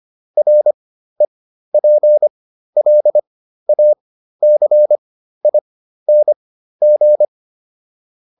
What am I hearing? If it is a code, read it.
Morse code, heard as a keyed tone, about 25 wpm, spaced as for 13 wpm, 600 Hz. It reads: REPLACING